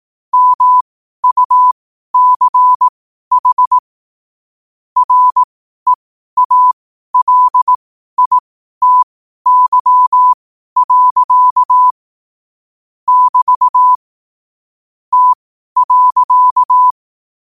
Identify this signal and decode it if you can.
Morse code, heard as a keyed tone, about 18 wpm, spaced as for 12 wpm, 1000 Hz. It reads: MUCH REALITY. <BT> T.
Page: Quotes for Thu, 14 Aug 2025 in Morse Code at 12 words per minute.